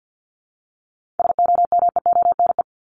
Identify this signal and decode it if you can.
Morse code, heard as a keyed tone, about 50 wpm, 700 Hz. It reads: SOMEONE
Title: Top 501-750 Words 50wpm